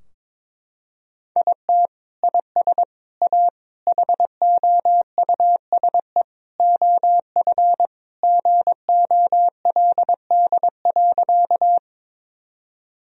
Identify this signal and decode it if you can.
Morse code, heard as a keyed tone, about 22 wpm, 700 Hz. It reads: IT IS A HOUSE OF GOLD.
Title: Sentences from Top 1000 Words - Review Part 3 - 22wpm